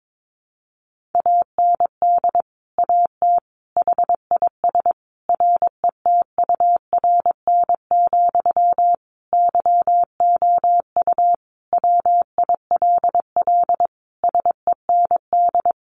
Morse code, heard as a keyed tone, about 22 wpm, 700 Hz. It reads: AND AT HIS RETURN, YOU WILL SEND
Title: Sentences from Top 1500 Words 22wpm